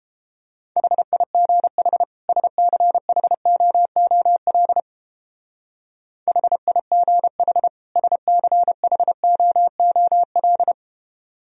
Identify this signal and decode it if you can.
Morse code, heard as a keyed tone, about 29 wpm, 700 Hz. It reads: HIGH SCHOOL HIGH SCHOOL